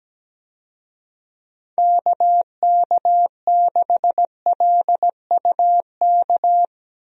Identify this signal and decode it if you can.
Morse code, heard as a keyed tone, about 17 wpm, 700 Hz. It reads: KK6LUK
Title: USA Call Signs - 6 Characters - 17wpm